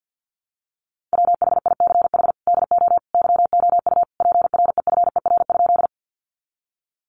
Morse code, heard as a keyed tone, about 50 wpm, 700 Hz. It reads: WHICH DO YOU PREFER?
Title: Sentences from Top 1100 Words 50wpm